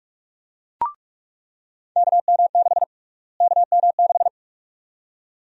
Morse code, heard as a keyed tone, about 46 wpm, 700 Hz. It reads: KM6 KM6
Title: USA Call Sign Prefixes - Repeated 2x - 45wpm